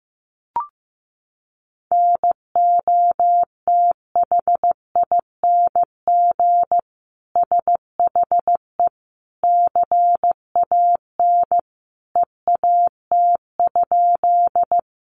Morse code, heard as a keyed tone, about 15 wpm, 700 Hz. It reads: NOTHING SHE CAN EAT?
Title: Sentences from Top 500 Words 15wpm